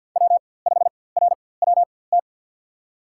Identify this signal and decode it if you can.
Morse code, heard as a keyed tone, about 47 wpm, spaced as for 23 wpm, 700 Hz. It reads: W5RWT
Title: USA Call Signs - 5 Characters - 4x Character Spacing - 50wpm